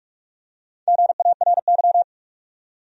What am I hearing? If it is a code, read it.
Morse code, heard as a keyed tone, about 46 wpm, 700 Hz. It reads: GARY
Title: USA Common Names - 45wpm